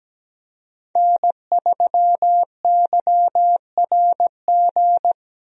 Morse code, heard as a keyed tone, about 17 wpm, 700 Hz. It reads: N3YRG
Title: USA Call Signs - 5 Characters - Repeated 3x - 17wpm